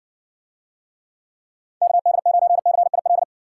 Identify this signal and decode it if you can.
Morse code, heard as a keyed tone, about 53 wpm, 700 Hz. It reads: KD9CIL